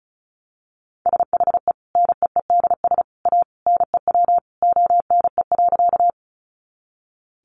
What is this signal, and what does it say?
Morse code, heard as a keyed tone, about 34 wpm, 700 Hz. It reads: SHE NEEDS A NEW ONE.
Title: Additional Easy Phrases 35wpm